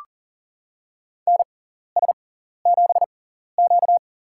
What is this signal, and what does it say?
Morse code, heard as a keyed tone, about 40 wpm, spaced as for 14 wpm, 700 Hz. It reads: NS7Q
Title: USA Call Signs - 4 Characters - 6x Character Spacing - 40wpm